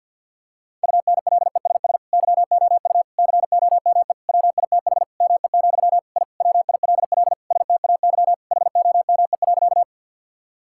Morse code, heard as a keyed tone, about 50 wpm, 700 Hz. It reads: UNLESS YOU COME WITH ME, I WILL STAY HOME.
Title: Sentences from Top 1000 Words - Review Part 6 - 50wpm